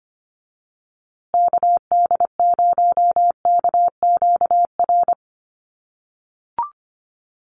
Morse code, heard as a keyed tone, about 25 wpm, 700 Hz. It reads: KD0KQR
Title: USA Call Signs Repeated 2x 25wpm